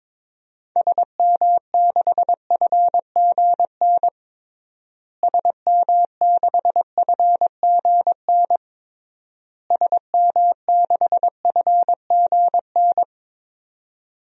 Morse code, heard as a keyed tone, about 22 wpm, 700 Hz. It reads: SM6FGN SM6FGN SM6FGN
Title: DXCC-100 Call Signs - 6 Characters - Repeated 3x - 22wpm